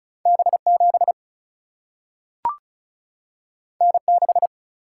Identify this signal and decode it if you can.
Morse code, heard as a keyed tone, about 36 wpm, 700 Hz. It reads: B 7 N 6
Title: Single-Letter Single-Number 35wpm